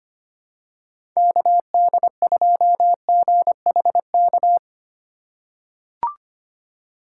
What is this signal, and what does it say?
Morse code, heard as a keyed tone, about 25 wpm, 700 Hz. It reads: KD2GHK
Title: USA Call Signs - 4x Character Spacing - 25wpm